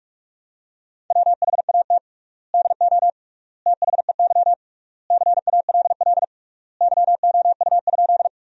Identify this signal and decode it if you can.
Morse code, heard as a keyed tone, about 46 wpm, 700 Hz. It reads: WHAT DO THEY CALL YOU?
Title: 3x Word-Spacing - Sentences from Top 100 Words - 45wpm